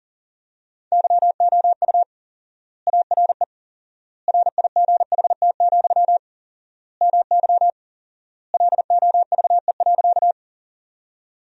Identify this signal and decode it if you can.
Morse code, heard as a keyed tone, about 40 wpm, 700 Hz. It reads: YOU ARE RIGHT, MY LOVE.
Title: 4x Word-Spacing - Sentences from Top 400 Words - 40wpm